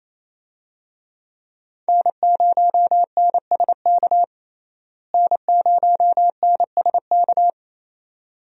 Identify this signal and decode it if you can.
Morse code, heard as a keyed tone, about 28 wpm, 700 Hz. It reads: N0NSK N0NSK
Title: USA Call Signs - 5 Characters - Repeated 2x - 28wpm